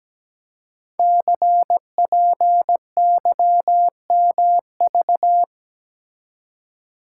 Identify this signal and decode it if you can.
Morse code, heard as a keyed tone, about 17 wpm, 700 Hz. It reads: CPYMV
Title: Code Groups - 5 Letter:Numbers - 17wpm